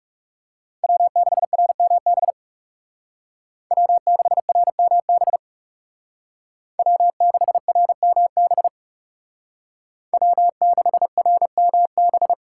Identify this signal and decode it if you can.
Morse code, heard as a keyed tone, about 38 wpm, 700 Hz. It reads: W6RMB W6RMB W6RMB W6RMB